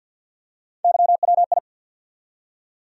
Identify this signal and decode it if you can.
Morse code, heard as a keyed tone, about 47 wpm, 700 Hz. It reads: YWI
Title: USA Call Sign Postfixes - 3 Letters 50wpm